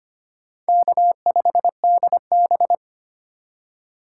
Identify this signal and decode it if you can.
Morse code, heard as a keyed tone, about 25 wpm, 700 Hz. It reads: K5DB
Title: USA Call Signs - 4 Characters - 3x Character Spacing - 25wpm